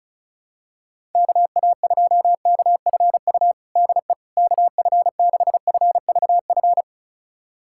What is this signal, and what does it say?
Morse code, heard as a keyed tone, about 36 wpm, 700 Hz. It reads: KA2KFU DE KF6FVF
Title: USA Callsign DE Callsign - 6 Characters - 35wpm